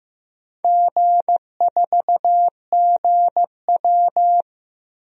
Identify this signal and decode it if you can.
Morse code, heard as a keyed tone, about 15 wpm, 700 Hz. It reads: G4GW
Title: DXCC-100 Call Signs - 4 Characters - Repeated 3x - 15wpm